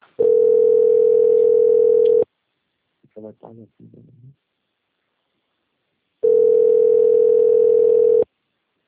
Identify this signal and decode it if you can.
Ringback tone (North America), 440 + 480 Hz, with speech.
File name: out- - - - - .32551